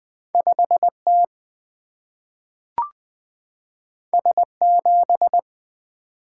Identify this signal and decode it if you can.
Morse code, heard as a keyed tone, about 20 wpm, 700 Hz. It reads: 5 T S 7
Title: Single-Letter Single-Number 20wpm